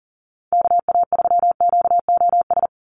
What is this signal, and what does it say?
Morse code, heard as a keyed tone, about 40 wpm, 700 Hz. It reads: KA3QOS
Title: USA Call Signs 40wpm